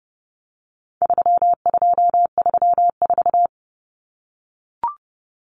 Morse code, heard as a keyed tone, about 30 wpm, 700 Hz. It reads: 3234
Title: Common Cut Numbers - 4-Digit 30wpm